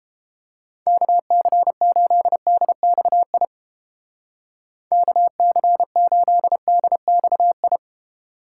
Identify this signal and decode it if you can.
Morse code, heard as a keyed tone, about 31 wpm, 700 Hz. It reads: KC8DXI KC8DXI